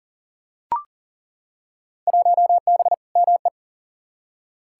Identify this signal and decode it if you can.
Morse code, heard as a keyed tone, about 38 wpm, 700 Hz. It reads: 1B ME
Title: Field Day Exchanges - 40wpm